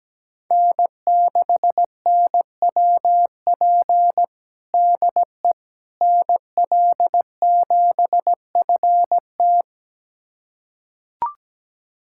Morse code, heard as a keyed tone, about 17 wpm, 700 Hz. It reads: N6NWP DE NL7FT
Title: USA Callsign DE Callsign - 5 Characters - 17wpm